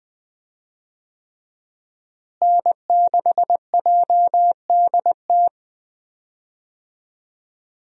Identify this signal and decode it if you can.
Morse code, heard as a keyed tone, about 20 wpm, 700 Hz. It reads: N6JDT
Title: USA Call Signs - 5 Characters - Repeated 3x - 20wpm